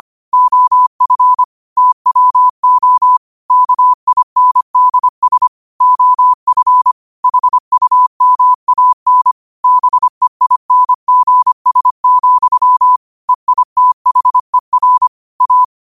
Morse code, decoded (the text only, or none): OF TWO KINDS OF HUMAN BEINGS, EITHER A